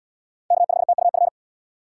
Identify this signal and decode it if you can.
Morse code, heard as a keyed tone, about 75 wpm, 700 Hz. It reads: K5TUF